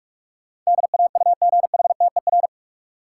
Morse code, heard as a keyed tone, about 46 wpm, 700 Hz. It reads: DAUGHTER
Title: Sentences from Top 1500 Words - New Word Drill - 45wpm